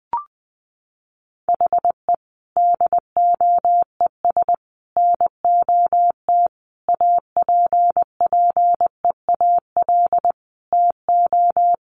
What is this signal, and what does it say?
Morse code, heard as a keyed tone, about 20 wpm, 700 Hz. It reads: HE DOES NOT APPEAL TO
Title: Sentences from Top 1300 Words 20wpm